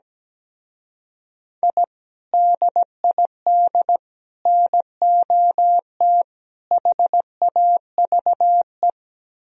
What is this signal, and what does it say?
Morse code, heard as a keyed tone, about 17 wpm, 700 Hz. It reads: I DID NOT HAVE
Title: Speed-Racing - Sets of 4 Words 17wpm